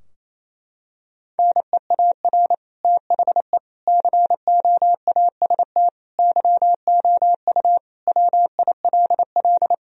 Morse code, heard as a keyed tone, about 27 wpm, 700 Hz. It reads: NEAR THE COAST YOU WILL
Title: Sentences from Top 1000 Words - Review Part 4 - 28wpm